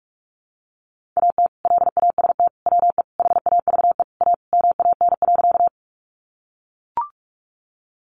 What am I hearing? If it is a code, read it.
Morse code, heard as a keyed tone, about 44 wpm, 700 Hz. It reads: AT LAST WE HAVE A MAN.
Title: Sentences from Top 300 Words 45wpm